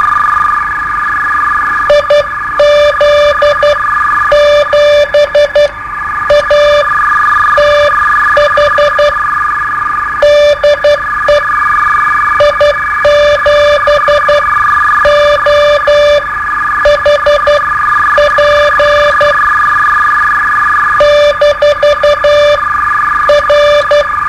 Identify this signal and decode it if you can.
Morse code, heard as a keyed tone, about 11 wpm, spaced as for 9 wpm, 590 Hz, with noise. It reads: IZ7ATH DE I7OHP -R
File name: qso-01